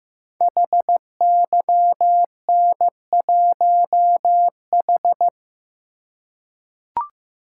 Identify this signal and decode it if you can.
Morse code, heard as a keyed tone, about 15 wpm, 700 Hz. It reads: HYN1H
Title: Code Groups - 5 Letter:Numbers - 15wpm